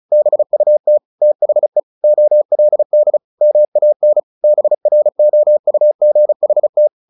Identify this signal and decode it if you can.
Morse code, heard as a keyed tone, about 34 wpm, 600 Hz. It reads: BUT THE OLD MAN BROUGHT
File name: the old man and the sea / the old man and the sea 1 35wpm